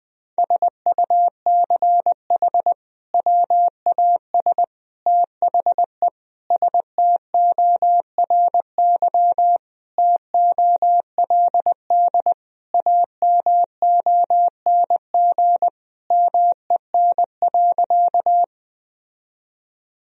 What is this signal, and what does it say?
Morse code, heard as a keyed tone, about 21 wpm, 700 Hz. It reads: SUCH WAS THE STORY TOLD AMONG MEN.
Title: Sentences from Top 1000 Words - Review Part 3 - 20wpm